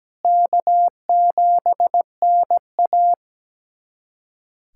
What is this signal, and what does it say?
Morse code, heard as a keyed tone, about 17 wpm, 700 Hz. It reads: K7NA
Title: USA Callsign DE Callsign - 4 Characters - 17wpm